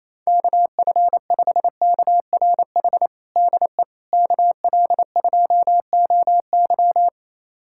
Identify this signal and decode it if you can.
Morse code, heard as a keyed tone, about 28 wpm, 700 Hz. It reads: KF5KRH DE KL2OY
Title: USA Callsign DE Callsign 28wpm